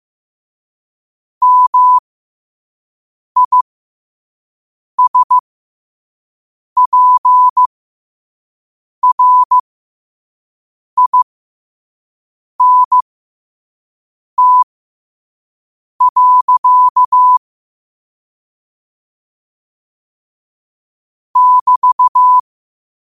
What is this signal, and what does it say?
Morse code, heard as a keyed tone, about 15 wpm, spaced as for 5 wpm, 1000 Hz. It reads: MISPRINT. <BT>
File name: qotdfiles / 5wpm